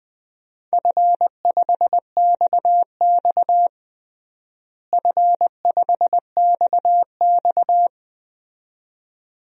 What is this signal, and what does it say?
Morse code, heard as a keyed tone, about 19 wpm, 700 Hz. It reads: F5XX F5XX
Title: DXCC-100 Call Signs - 4 Characters - Repeated 2x - 20wpm